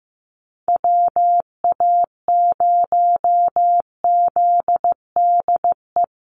Morse code, heard as a keyed tone, about 15 wpm, 700 Hz. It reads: WA0ZDE
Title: USA Call Signs Repeated 3x 15wpm